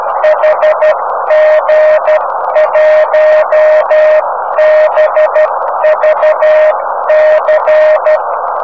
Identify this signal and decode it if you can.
Morse code, heard as a keyed tone, about 12 wpm, 620 Hz, with noise. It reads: HG1BVC
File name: om3pv 70mhz / zaznam / hg1bvc